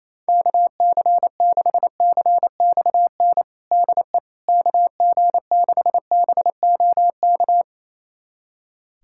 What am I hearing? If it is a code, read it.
Morse code, heard as a keyed tone, about 28 wpm, 700 Hz. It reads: KC6CXN DE KG6BOK
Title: USA Callsign DE Callsign - 6 Characters - 28wpm